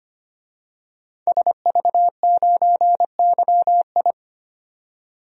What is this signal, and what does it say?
Morse code, heard as a keyed tone, about 25 wpm, 700 Hz. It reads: SV9YI
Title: DXCC-100 Call Signs - 5 Characters - 25wpm